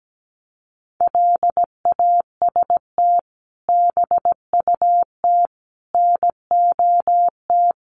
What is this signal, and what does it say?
Morse code, heard as a keyed tone, about 17 wpm, 700 Hz. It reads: LAST BUT NOT
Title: Sentences from Top 700 Words 17wpm